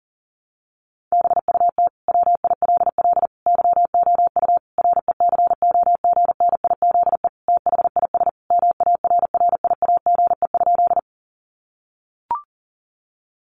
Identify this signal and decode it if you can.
Morse code, heard as a keyed tone, about 40 wpm, 700 Hz. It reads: BUT WILL YOU RECOGNIZE THIS MARRIAGE?
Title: Sentences from Top 1400 Words 40wpm